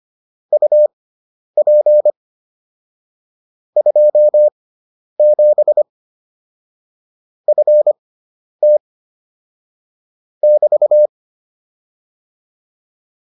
Morse code, heard as a keyed tone, about 25 wpm, spaced as for 10 wpm, 600 Hz. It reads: UP 27 FT <BT>